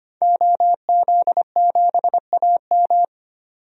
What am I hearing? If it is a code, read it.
Morse code, heard as a keyed tone, about 25 wpm, 700 Hz. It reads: OZ7AM
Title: DXCC-100 Call Signs - 5 Characters - 25wpm